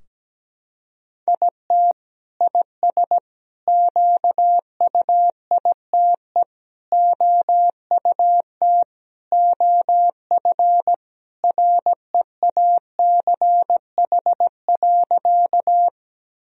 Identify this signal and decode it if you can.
Morse code, heard as a keyed tone, about 17 wpm, 700 Hz. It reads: IT IS QUITE OUT OF REACH.
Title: Sentences from Top 1000 Words - Review Part 2 - 17wpm